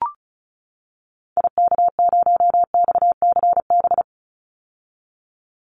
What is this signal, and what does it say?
Morse code, heard as a keyed tone, about 34 wpm, 700 Hz. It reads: IK0XCB
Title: Call Signs - 35wpm